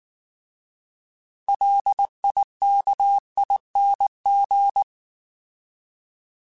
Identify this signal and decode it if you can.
Morse code, heard as a keyed tone, about 19 wpm, 800 Hz. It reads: LIKING